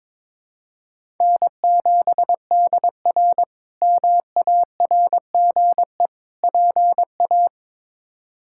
Morse code, heard as a keyed tone, about 21 wpm, 700 Hz. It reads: N7DR MARGE PA
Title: K1USN SST QSO Practice - 22wpm